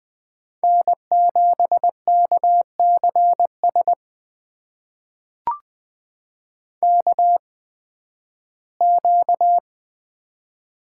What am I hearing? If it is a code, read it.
Morse code, heard as a keyed tone, about 20 wpm, spaced as for 13 wpm, 700 Hz. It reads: N7KCS KQ
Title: USA Call Signs - 5 Characters - 8x Character Spacing - 20wpm